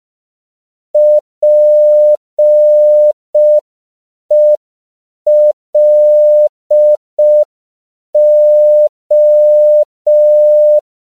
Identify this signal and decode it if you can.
Morse code, heard as a keyed tone, about 5 wpm, 600 Hz. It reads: PELO
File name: CW / 5PPM